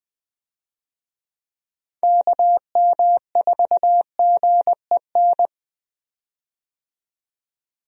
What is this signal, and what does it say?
Morse code, heard as a keyed tone, about 20 wpm, 700 Hz. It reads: KM4GEN
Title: USA Call Signs - 6 Characters - 20wpm